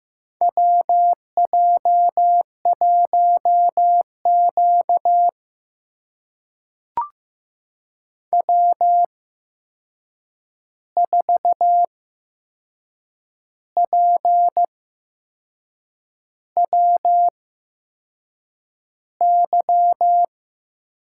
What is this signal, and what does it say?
Morse code, heard as a keyed tone, about 15 wpm, spaced as for 6 wpm, 700 Hz. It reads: WJ1Q W4PWY
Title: USA Call Signs - 8x Character Spacing - 15wpm